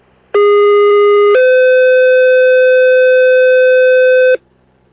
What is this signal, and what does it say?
Two-tone page, heard as a one-second tone, then a three-second tone, about 410 then 510 Hz, with noise. It is heard from 0.3 to 4.3 s.